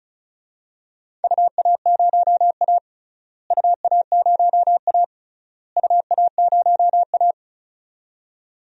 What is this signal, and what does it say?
Morse code, heard as a keyed tone, about 34 wpm, 700 Hz. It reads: UA0A UA0A UA0A
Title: DXCC-100 Call Signs - 4 Characters - Repeated 3x - 35wpm